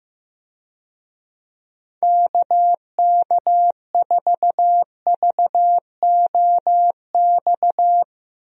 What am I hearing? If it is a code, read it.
Morse code, heard as a keyed tone, about 16 wpm, 700 Hz. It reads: KK4VOX
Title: USA Call Signs - 6 Characters - Repeated 2x - 15wpm